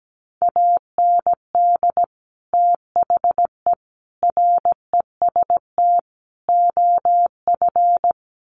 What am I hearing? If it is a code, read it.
Morse code, heard as a keyed tone, about 17 wpm, 700 Hz. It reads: AND THE REST OF
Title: Sets of 4 Words 17wpm